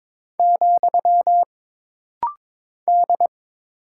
Morse code, heard as a keyed tone, about 22 wpm, 700 Hz. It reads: , D
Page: This rapid-fire timing is geared to push your ICR (Instant Character Recognition) skills.